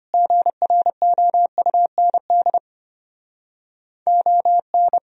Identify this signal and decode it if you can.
Morse code, heard as a keyed tone, about 30 wpm, 700 Hz. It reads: GROUND ON
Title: Speed-Racing - Sets of 3 Words 25wpm